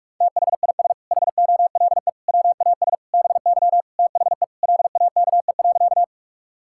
Morse code, heard as a keyed tone, about 46 wpm, 700 Hz. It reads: THIS HOLE WAS BY THE LAKE.
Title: Sentences from Top 1600 Words 45wpm